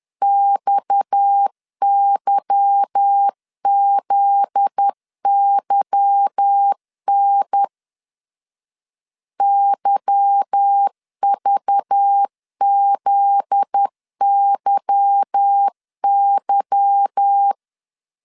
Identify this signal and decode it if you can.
Morse code, heard as a keyed tone, about 10 wpm, 800 Hz. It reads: XYZYN YVZYY
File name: LEZIONI / 8 YJ45 / 8 Y 10wpm 2615